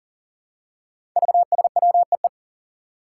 Morse code, heard as a keyed tone, about 40 wpm, 700 Hz. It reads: VSWEE
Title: Five Challenging Letter-Number Combinations 40wpm